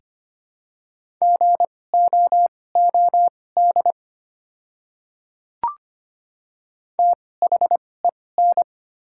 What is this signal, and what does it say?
Morse code, heard as a keyed tone, about 25 wpm, spaced as for 18 wpm, 700 Hz. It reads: GOOD THEN
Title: 2x Character Spacing - Top 300 Words - 25wpm